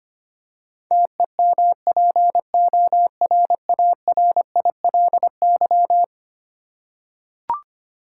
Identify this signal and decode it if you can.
Morse code, heard as a keyed tone, about 25 wpm, 700 Hz. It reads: TEMPORARILY
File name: practice / 25wpm / Eleven Letter Words - 25wpm